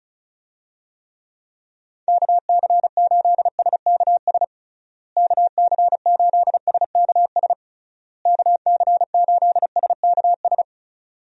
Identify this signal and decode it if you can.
Morse code, heard as a keyed tone, about 34 wpm, 700 Hz. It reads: KC8SKS KC8SKS KC8SKS
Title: USA Call Signs - 6 Characters - Repeated 3x - 35wpm